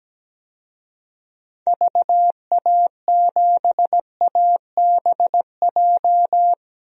Morse code, heard as a keyed tone, about 17 wpm, 700 Hz. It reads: VA7ABJ
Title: DXCC-100 Call Signs - 6 Characters - 17wpm